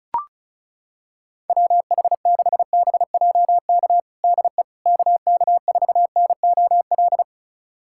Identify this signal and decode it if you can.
Morse code, heard as a keyed tone, about 34 wpm, 700 Hz. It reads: WH6BJK DE KK4NOL
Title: USA Callsign DE Callsign 35wpm